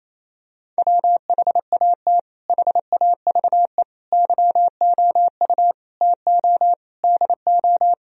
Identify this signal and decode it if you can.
Morse code, heard as a keyed tone, about 28 wpm, 700 Hz. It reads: WHAT HAVE YOU TO DO
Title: Sentences from Top 1000 Words - Review Part 2 - 28wpm